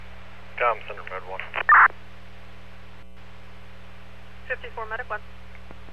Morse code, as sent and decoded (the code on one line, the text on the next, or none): none